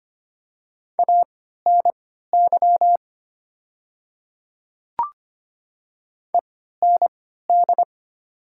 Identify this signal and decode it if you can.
Morse code, heard as a keyed tone, about 25 wpm, spaced as for 14 wpm, 700 Hz. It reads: ANY END
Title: 3x Character Spacing - Top 200 Words - 25wpm